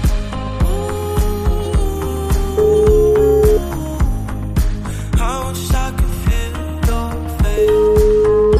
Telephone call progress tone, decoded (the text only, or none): ringback tone